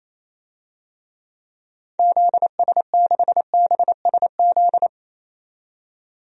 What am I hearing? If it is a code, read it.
Morse code, heard as a keyed tone, about 28 wpm, 700 Hz. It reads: ZS6BSZ
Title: DXCC-100 Call Signs - 6 Characters - 28wpm